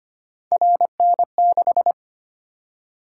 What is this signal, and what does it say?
Morse code, heard as a keyed tone, about 25 wpm, 700 Hz. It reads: RN6
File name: Three Letter-Number Combinations - 25wpm